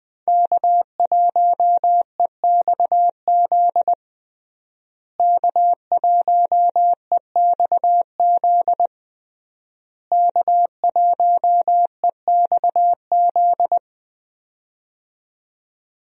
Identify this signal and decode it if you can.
Morse code, heard as a keyed tone, about 20 wpm, 700 Hz. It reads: K1EXZ K1EXZ K1EXZ
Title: USA Call Signs - 5 Characters - Repeated 3x - 20wpm